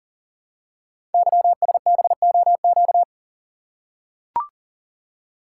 Morse code, heard as a keyed tone, about 42 wpm, 690 Hz. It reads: YSBOQ
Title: Code Groups - 5 Letters - 40wpm